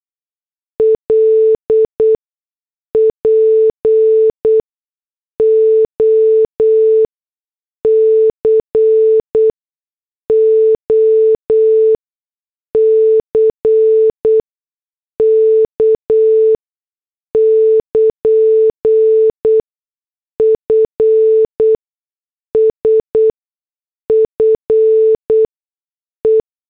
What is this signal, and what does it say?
Morse code, heard as a keyed tone, about 8 wpm, spaced as for 6 wpm, 440 Hz. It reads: LPOCOCK <KN>FSFE